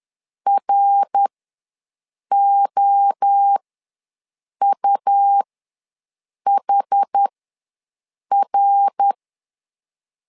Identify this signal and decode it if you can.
Morse code, heard as a keyed tone, about 10 wpm, spaced as for 6 wpm, 800 Hz. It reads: ROUHR
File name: LEZIONI / 3 MWRO / 3 MWRO ws 10wpm 2310